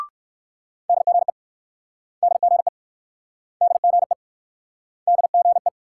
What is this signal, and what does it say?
Morse code, heard as a keyed tone, about 59 wpm, 690 Hz. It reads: DGE DGE DGE DGE